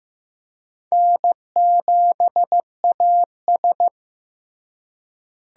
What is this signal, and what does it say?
Morse code, heard as a keyed tone, about 15 wpm, 700 Hz. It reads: N7AS
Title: USA Call Signs - 4 Characters - Repeated 3x - 15wpm